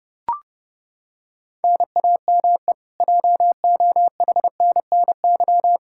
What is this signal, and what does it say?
Morse code, heard as a keyed tone, about 29 wpm, 700 Hz. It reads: NAME JOHNNY
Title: Practice QSO - Exchange 1 - Beginning - 30wpm